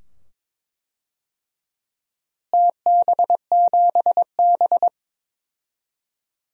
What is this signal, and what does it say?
Morse code, heard as a keyed tone, about 22 wpm, 700 Hz. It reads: TB7B
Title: DXCC-100 Call Signs - 4 Characters - Repeated 2x - 22wpm